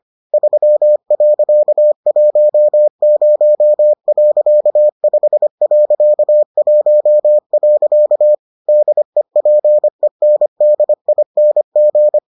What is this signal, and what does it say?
Morse code, heard as a keyed tone, about 25 wpm, 600 Hz. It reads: 3.10.5.1. DEPENDING